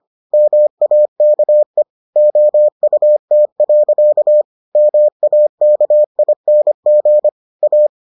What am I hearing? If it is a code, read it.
Morse code, heard as a keyed tone, about 25 wpm, 600 Hz. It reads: MAKE OUT. MAKING A